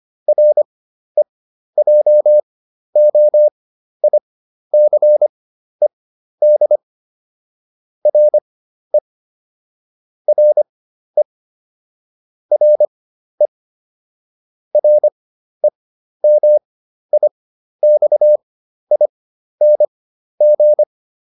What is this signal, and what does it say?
Morse code, heard as a keyed tone, about 25 wpm, spaced as for 12 wpm, 600 Hz. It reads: REJOICED RE RE RE REMIXING